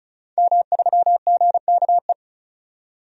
Morse code, heard as a keyed tone, about 36 wpm, 700 Hz. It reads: M3GKE
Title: DXCC-100 Call Signs - 5 Characters - 35wpm